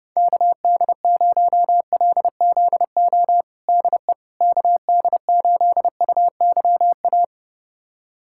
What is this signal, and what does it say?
Morse code, heard as a keyed tone, about 32 wpm, 700 Hz. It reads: KD0LZO DE KD8UYA
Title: USA Callsign DE Callsign - 6 Characters - 30wpm